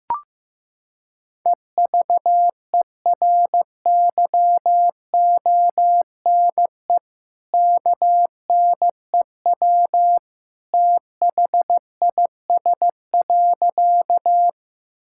Morse code, heard as a keyed tone, about 15 wpm, 700 Hz. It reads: EVERYONE KNEW THIS.
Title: Sentences from Top 1000 Words - Review Part 5 - 15wpm